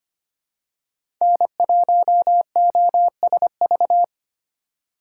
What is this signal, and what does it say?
Morse code, heard as a keyed tone, about 25 wpm, 700 Hz. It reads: N1OSV
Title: USA Call Signs - 5 Characters - 2x Character Spacing - 25wpm